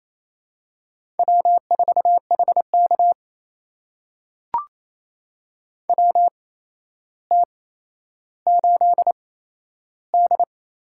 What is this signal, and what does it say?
Morse code, heard as a keyed tone, about 27 wpm, spaced as for 12 wpm, 700 Hz. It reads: W4HK WT8D
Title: USA Call Signs - 4 Characters - 8x Character Spacing - 28wpm